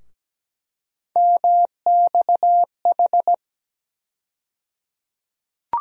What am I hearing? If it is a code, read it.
Morse code, heard as a keyed tone, about 17 wpm, 700 Hz. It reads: MXH
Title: USA Call Sign Postfixes - 3 Letters 17wpm